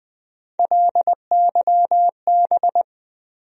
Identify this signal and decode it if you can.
Morse code, heard as a keyed tone, about 20 wpm, 700 Hz. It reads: LYB
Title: USA Call Sign Postfixes - 3 Letters 20wpm